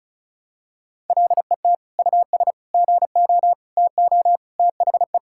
Morse code, heard as a keyed tone, about 36 wpm, 700 Hz. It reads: LET US GO TO THE
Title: Sentences from Top 1000 Words - Review Part 5 - 35wpm